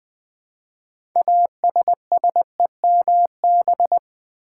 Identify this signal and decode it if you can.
Morse code, heard as a keyed tone, about 20 wpm, 700 Hz. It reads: ASSEMB
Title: Build A Word - 2 Letters At A Time - 8-Letter Words - 20wpm